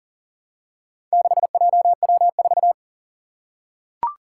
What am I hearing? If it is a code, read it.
Morse code, heard as a keyed tone, about 38 wpm, 700 Hz. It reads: 6JW4
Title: Four Challenging Letter-Number Combinations 40wpm